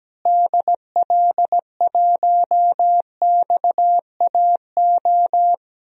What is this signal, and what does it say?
Morse code, heard as a keyed tone, about 17 wpm, 700 Hz. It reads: DL1XAO
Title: DXCC-100 Call Signs - 6 Characters - Repeated 2x - 17wpm